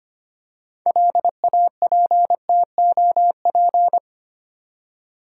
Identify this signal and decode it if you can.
Morse code, heard as a keyed tone, about 25 wpm, 690 Hz. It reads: LAPTOP
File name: practice / 25wpm / Easy Words with Hints - 25wpm